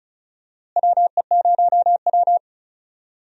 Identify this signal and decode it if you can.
Morse code, heard as a keyed tone, about 36 wpm, 700 Hz. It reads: WE0W
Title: USA Call Signs - 4 Characters - Repeated 3x - 35wpm